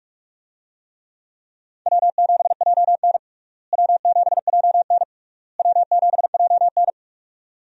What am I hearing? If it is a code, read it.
Morse code, heard as a keyed tone, about 46 wpm, 700 Hz. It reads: W7JN W7JN W7JN
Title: USA Call Signs - 4 Characters - Repeated 3x - 45wpm